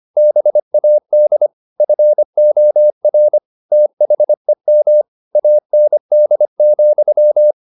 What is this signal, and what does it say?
Morse code, heard as a keyed tone, about 25 wpm, 600 Hz. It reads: BAD FOR THEM AND,
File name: the old man and the sea part 1 25wpm